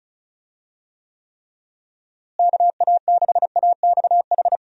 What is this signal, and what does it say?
Morse code, heard as a keyed tone, about 36 wpm, 700 Hz. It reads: KA6AXH
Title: USA Call Signs - 6 Characters - Repeated 3x - 35wpm